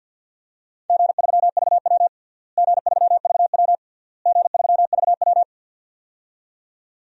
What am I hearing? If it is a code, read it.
Morse code, heard as a keyed tone, about 51 wpm, 700 Hz. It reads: G3VW G3VW G3VW
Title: DXCC-100 Call Signs - 4 Characters - Repeated 3x - 50wpm